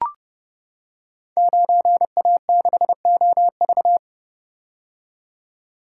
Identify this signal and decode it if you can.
Morse code, heard as a keyed tone, about 30 wpm, 700 Hz. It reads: 9A6OV
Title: Code Groups - 5 Letter:Numbers - 30wpm